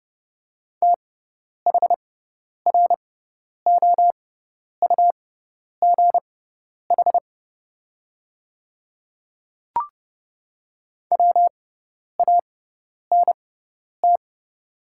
Morse code, heard as a keyed tone, about 30 wpm, spaced as for 10 wpm, 700 Hz. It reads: THROUGH WANT
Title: 6x Character Spacing - Top 300 Words - 30wpm